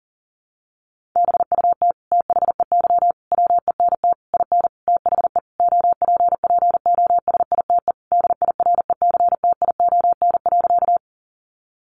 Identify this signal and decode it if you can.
Morse code, heard as a keyed tone, about 40 wpm, 700 Hz. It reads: BUT THEY WENT IN THE OPPOSITE DIRECTION.
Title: Sentences from Top 1300 Words 40wpm